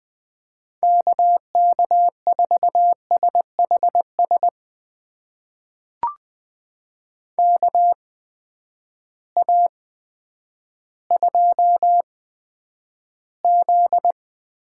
Morse code, heard as a keyed tone, about 20 wpm, spaced as for 10 wpm, 700 Hz. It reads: KK4SHS KA2Z
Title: USA Call Signs - 6 Characters - 8x Character Spacing - 20wpm